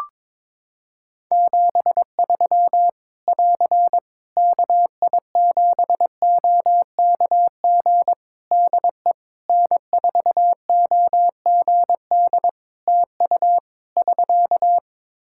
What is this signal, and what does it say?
Morse code, heard as a keyed tone, about 22 wpm, 700 Hz. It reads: 73 <AR> KI7OKG DE N4OGD TU <SK>
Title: K1USN SST QSO Practice 22wpm